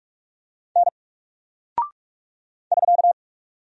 Morse code, heard as a keyed tone, about 44 wpm, 700 Hz. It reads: N <SK>
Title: Single Character (Letter:Number:Punctuation:Prosign) 45wpm